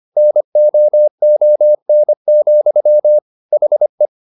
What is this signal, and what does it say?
Morse code, heard as a keyed tone, about 25 wpm, 600 Hz. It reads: NOON, HE